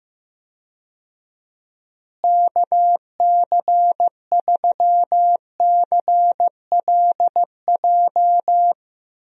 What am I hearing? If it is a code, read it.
Morse code, heard as a keyed tone, about 15 wpm, 700 Hz. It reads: KC3CLJ
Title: USA Call Signs - 6 Characters - Repeated 3x - 15wpm